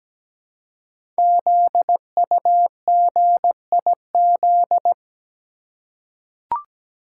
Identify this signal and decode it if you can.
Morse code, heard as a keyed tone, about 17 wpm, 700 Hz. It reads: ZUGIZ
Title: Code Groups - 5 Letter:Numbers - 17wpm